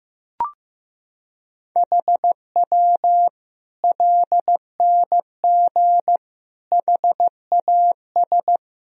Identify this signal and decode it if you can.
Morse code, heard as a keyed tone, about 16 wpm, 700 Hz. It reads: HW LNG HAS
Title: Sentences from Top 100 Words and Abbreviations 15wpm